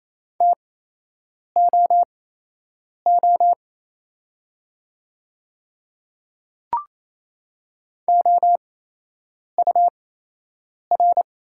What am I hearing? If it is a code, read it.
Morse code, heard as a keyed tone, about 28 wpm, spaced as for 8 wpm, 700 Hz. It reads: TOO OUR
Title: 8x Character Spacing - Top 300 Words - 28wpm